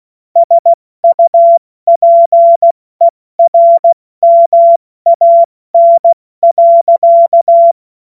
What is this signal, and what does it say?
Morse code, heard as a keyed tone, about 16 wpm, 680 Hz. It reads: SUPERMAN.
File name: Morsekurs / Witze / Witz 24. August 2021, 70 F80